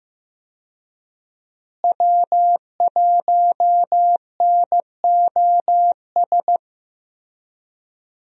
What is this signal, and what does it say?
Morse code, heard as a keyed tone, about 15 wpm, 700 Hz. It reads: W1NOS
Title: USA Call Signs - 5 Characters - Repeated 2x - 15wpm